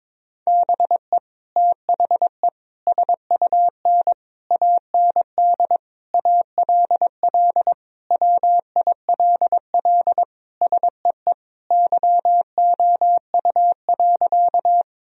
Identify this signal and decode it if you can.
Morse code, heard as a keyed tone, about 22 wpm, 700 Hz. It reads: BE THE SUN AND ALL WILL SEE YOU.
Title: Sentences from Top 1000 Words - Review Part 1 - 22wpm